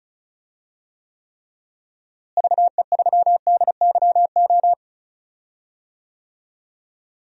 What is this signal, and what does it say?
Morse code, heard as a keyed tone, about 36 wpm, 700 Hz. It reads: VE3DYO
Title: DXCC-100 Call Signs - 6 Characters - Repeated 3x - 35wpm